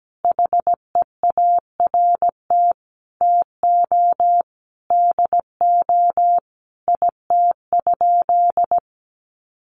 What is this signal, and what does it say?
Morse code, heard as a keyed tone, about 17 wpm, 700 Hz. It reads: HEART TO DO IT?
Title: Sentences from Top 1200 Words 17wpm